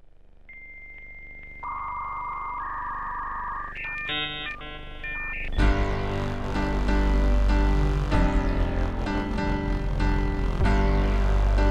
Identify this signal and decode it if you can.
Dial-up modem handshake.